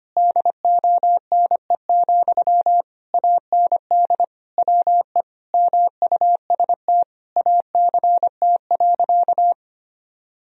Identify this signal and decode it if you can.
Morse code, heard as a keyed tone, about 25 wpm, 700 Hz. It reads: DONE, AND WE MUST ACT.
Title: Sentences from Top 1000 Words - Review Part 1 - 25wpm